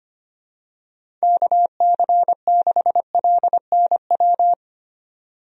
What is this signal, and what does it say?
Morse code, heard as a keyed tone, about 25 wpm, 700 Hz. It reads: KC6LNW
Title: USA Call Signs - 6 Characters - 25wpm